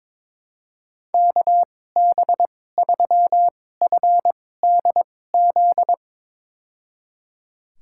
Morse code, heard as a keyed tone, about 22 wpm, spaced as for 16 wpm, 700 Hz. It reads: KB3FDZ
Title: USA Call Signs - 6 Characters - 2x Character Spacing - 22wpm